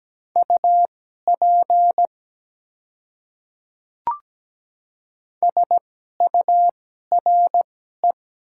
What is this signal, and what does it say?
Morse code, heard as a keyed tone, about 17 wpm, spaced as for 12 wpm, 700 Hz. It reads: UP SURE
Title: 2x Character Spacing - Top 300 Words - 17wpm